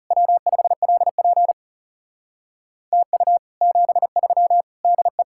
Morse code, heard as a keyed tone, about 39 wpm, 700 Hz. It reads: W5LP TU 73 DE